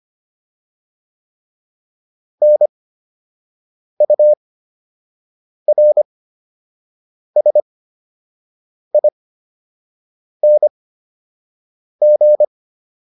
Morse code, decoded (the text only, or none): NURSING